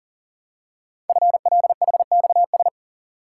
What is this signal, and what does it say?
Morse code, heard as a keyed tone, about 42 wpm, 700 Hz. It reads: FLHXS
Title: Code Groups - 5 Letters - 40wpm